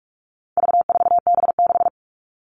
Morse code, heard as a keyed tone, about 44 wpm, 700 Hz. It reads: V4B6
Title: All Cut Numbers - 4-Digit 45wpm